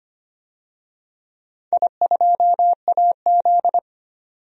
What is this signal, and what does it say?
Morse code, heard as a keyed tone, about 25 wpm, 700 Hz. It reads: I2AZ
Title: DXCC-100 Call Signs - 4 Characters - 25wpm